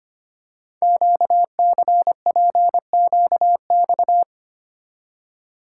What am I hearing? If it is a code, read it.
Morse code, heard as a keyed tone, about 25 wpm, 700 Hz. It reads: QCPQX
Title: Code Groups - 5 Letters - 25wpm